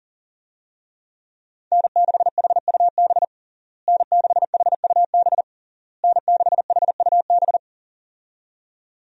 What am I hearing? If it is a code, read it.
Morse code, heard as a keyed tone, about 40 wpm, 700 Hz. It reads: N6HUB N6HUB N6HUB
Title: USA Call Signs - 5 Characters - Repeated 3x - 40wpm